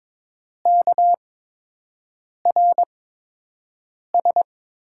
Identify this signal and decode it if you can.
Morse code, heard as a keyed tone, about 22 wpm, spaced as for 6 wpm, 700 Hz. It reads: KRS
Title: USA Call Signs - 5 Characters - 8x Character Spacing - 22wpm